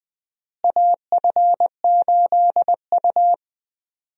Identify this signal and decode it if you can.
Morse code, heard as a keyed tone, about 20 wpm, 700 Hz. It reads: AF8U
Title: USA Call Signs - 4 Characters - Repeated 2x - 20wpm